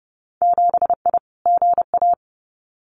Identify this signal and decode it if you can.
Morse code, heard as a keyed tone, about 32 wpm, 700 Hz. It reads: 7I GA
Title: Winter Field Day Exchanges Encore 30wpm